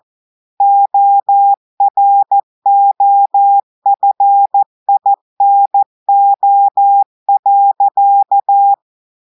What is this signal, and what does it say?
Morse code, heard as a keyed tone, about 14 wpm, 800 Hz. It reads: OROFINO.
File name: audios / portugais / portugais texte en clair 1 vitesse 14 mots minutes